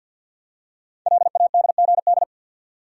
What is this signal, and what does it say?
Morse code, heard as a keyed tone, about 48 wpm, 700 Hz. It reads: LADGD
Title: Code Groups - 5 Letters - 50wpm